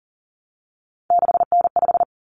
Morse code, heard as a keyed tone, about 40 wpm, 700 Hz. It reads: 6N5
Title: Common Cut Numbers - 3-Digit 40wpm